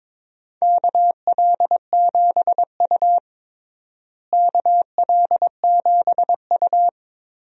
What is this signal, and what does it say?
Morse code, heard as a keyed tone, about 21 wpm, 700 Hz. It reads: KL7U KL7U
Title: DXCC-100 Call Signs - 4 Characters - Repeated 2x - 22wpm